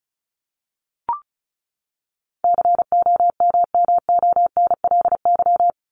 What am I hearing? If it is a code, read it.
Morse code, heard as a keyed tone, about 34 wpm, 700 Hz. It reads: COMMONLY
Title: LY Suffix - Top 100 Words with Base Words 35wpm